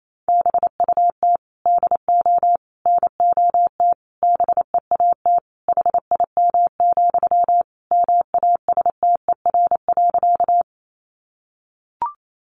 Morse code, heard as a keyed tone, about 28 wpm, 700 Hz. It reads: BUT DO NOT BEAT HIM, MASTER.
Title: Sentences from Top 1300 Words 28wpm